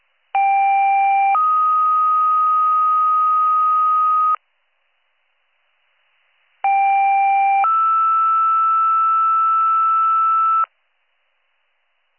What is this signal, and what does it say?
Two-tone page, heard as a one-second tone, then a three-second tone, about 790 then 1250 Hz, about 790 then 1320 Hz. It is heard from 0.3 to 10.6 s.